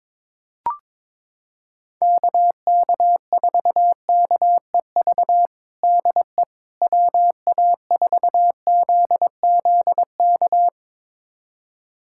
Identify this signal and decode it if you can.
Morse code, heard as a keyed tone, about 22 wpm, 700 Hz. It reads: KK4KEV DE WA4ZZK
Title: USA Callsign DE Callsign - 6 Characters - 22wpm